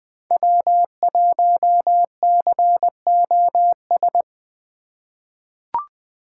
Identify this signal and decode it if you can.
Morse code, heard as a keyed tone, about 20 wpm, 700 Hz. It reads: W1COS
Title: USA Call Signs - 5 Characters - 2x Character Spacing - 20wpm